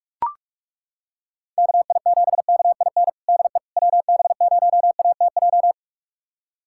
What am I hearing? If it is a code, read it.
Morse code, heard as a keyed tone, about 42 wpm, 700 Hz. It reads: KI7KIN DE WB0ATJ
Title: USA Callsign DE Callsign 45wpm